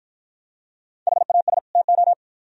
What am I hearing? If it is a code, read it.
Morse code, heard as a keyed tone, about 52 wpm, 700 Hz. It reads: HAS TO